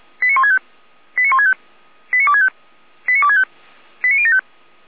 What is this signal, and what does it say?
Five-tone selective call (ZVEI), heard as quick runs of five tones, tones of about 80 ms. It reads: 89165 89165 89265 89265 89075